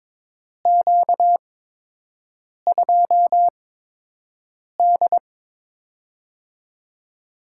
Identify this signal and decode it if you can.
Morse code, heard as a keyed tone, about 22 wpm, spaced as for 6 wpm, 700 Hz. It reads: Q2D
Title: USA Call Signs - 4 Characters - 8x Character Spacing - 22wpm